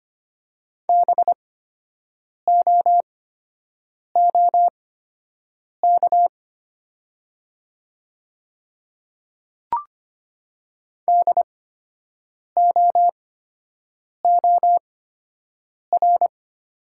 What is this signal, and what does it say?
Morse code, heard as a keyed tone, about 25 wpm, spaced as for 7 wpm, 700 Hz. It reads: BOOK DOOR
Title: 8x Character Spacing - Top 300 Words - 25wpm